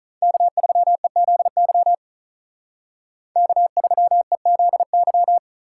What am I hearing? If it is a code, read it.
Morse code, heard as a keyed tone, about 44 wpm, 700 Hz. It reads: K3EZY K3EZY